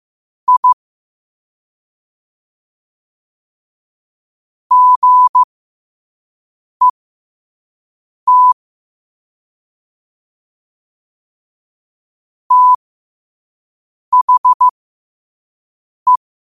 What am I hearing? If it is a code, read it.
Morse code, heard as a keyed tone, about 15 wpm, spaced as for 5 wpm, 1000 Hz. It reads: I GET THE
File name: qotdfiles / 5wpm